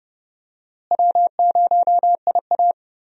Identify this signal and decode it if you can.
Morse code, heard as a keyed tone, about 31 wpm, 700 Hz. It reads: W0IA
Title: USA Call Signs - 4 Characters - Repeated 3x - 30wpm